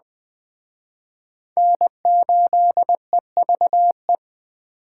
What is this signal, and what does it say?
Morse code, heard as a keyed tone, about 21 wpm, 700 Hz. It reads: N8EVE